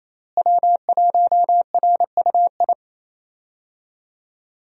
Morse code, heard as a keyed tone, about 28 wpm, 700 Hz. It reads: W1RUI
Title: USA Call Signs - 5 Characters - Repeated 2x - 28wpm